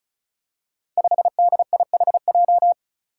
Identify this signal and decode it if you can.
Morse code, heard as a keyed tone, about 34 wpm, 700 Hz. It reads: 5DIHJ
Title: Five Challenging Letter-Number Combinations 35wpm